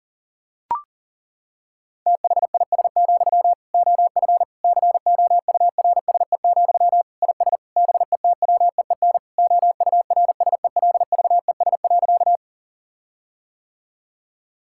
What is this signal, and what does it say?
Morse code, heard as a keyed tone, about 40 wpm, 700 Hz. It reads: THIS, OF COURSE, IS BETWEEN OURSELVES.
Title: Sentences from Top 1600 Words 40wpm